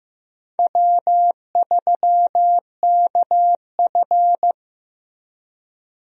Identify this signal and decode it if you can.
Morse code, heard as a keyed tone, about 15 wpm, 700 Hz. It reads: W3KF
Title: USA Call Signs - 4 Characters - Repeated 3x - 15wpm